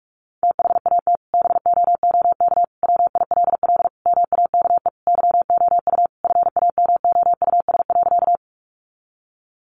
Morse code, heard as a keyed tone, about 44 wpm, 700 Hz. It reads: THAT BOOK WILL MAKE YOU FAMOUS.
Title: Sentences from Top 1500 Words 45wpm